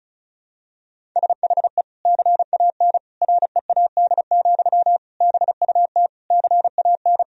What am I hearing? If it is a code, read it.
Morse code, heard as a keyed tone, about 34 wpm, 700 Hz. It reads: SHE CAN READ, BUT CAN
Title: Sentences from Top 1000 Words - Review Part 1 - 35wpm